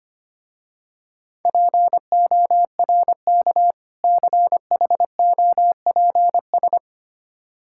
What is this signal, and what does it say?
Morse code, heard as a keyed tone, about 25 wpm, 700 Hz. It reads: PORK CHOPS
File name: practice / 25wpm / Difficult Words with Hints - 25wpm